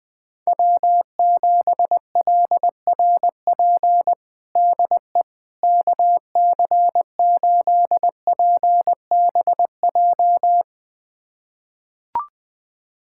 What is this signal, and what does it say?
Morse code, heard as a keyed tone, about 21 wpm, 700 Hz. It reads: W7LRP DE KC8PBJ
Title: USA Callsign DE Callsign 20wpm